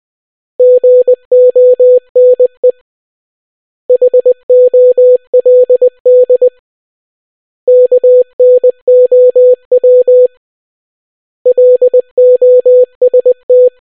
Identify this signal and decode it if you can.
Morse code, heard as a keyed tone, about 20 wpm, 500 Hz. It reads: GONE HOLD KNOW LOST